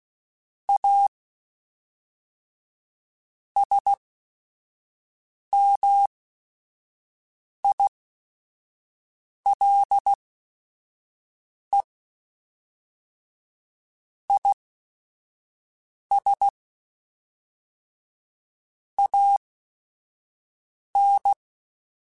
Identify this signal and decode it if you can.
Morse code, heard as a keyed tone, about 16 wpm, spaced as for 5 wpm, 800 Hz. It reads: A SMILE IS AN